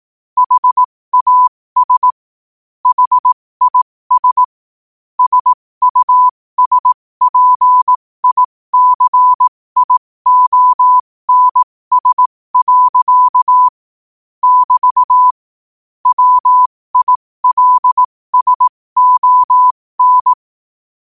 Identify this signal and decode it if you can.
Morse code, heard as a keyed tone, about 18 wpm, spaced as for 15 wpm, 1000 Hz. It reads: HAS HIS SUSPICIONS. <BT> WILSON
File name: qotdfiles / 15wpm